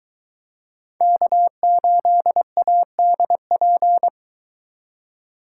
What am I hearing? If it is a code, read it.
Morse code, heard as a keyed tone, about 23 wpm, 700 Hz. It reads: K8ADP